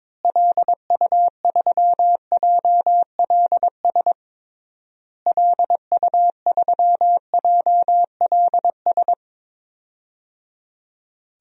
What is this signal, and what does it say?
Morse code, heard as a keyed tone, about 21 wpm, 700 Hz. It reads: LU3JLS LU3JLS
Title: DXCC-100 Call Signs - 6 Characters - Repeated 3x - 22wpm